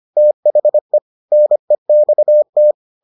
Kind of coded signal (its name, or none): Morse code